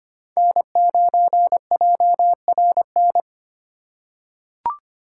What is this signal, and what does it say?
Morse code, heard as a keyed tone, about 25 wpm, 700 Hz. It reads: N9JRN
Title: USA Call Signs - 4x Character Spacing - 25wpm